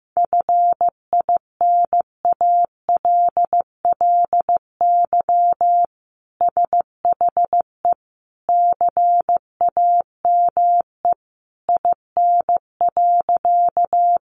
Morse code, heard as a keyed tone, about 15 wpm, 700 Hz. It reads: FINALLY SHE CAME IN.
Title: Sentences from Top 1000 Words 15wpm